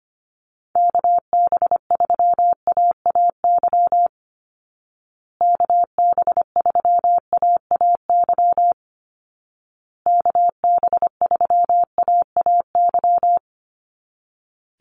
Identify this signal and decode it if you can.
Morse code, heard as a keyed tone, about 25 wpm, 700 Hz. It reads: KB3AAY KB3AAY KB3AAY
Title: Call Signs Repeated 2x - 25wpm